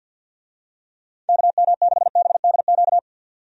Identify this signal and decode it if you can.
Morse code, heard as a keyed tone, about 51 wpm, 700 Hz. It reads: KM6BDQ
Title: USA Call Signs - 6 Characters - 50wpm